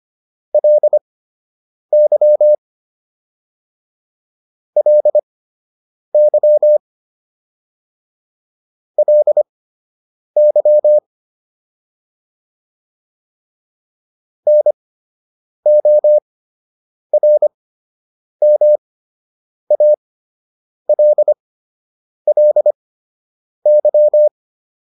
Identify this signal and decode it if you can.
Morse code, heard as a keyed tone, about 25 wpm, spaced as for 8 wpm, 600 Hz. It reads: LY LY LY NORMALLY